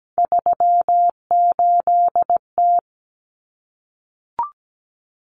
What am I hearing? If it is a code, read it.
Morse code, heard as a keyed tone, about 17 wpm, 700 Hz. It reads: 38T
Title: Common Cut Numbers - 3-Digit 17wpm